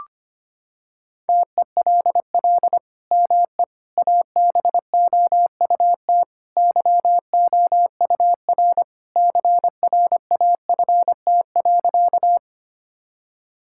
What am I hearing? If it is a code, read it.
Morse code, heard as a keyed tone, about 25 wpm, 700 Hz. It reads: TELL ME ABOUT YOUR CRAFT.
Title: Sentences from Top 1000 Words - Review Part 6 - 25wpm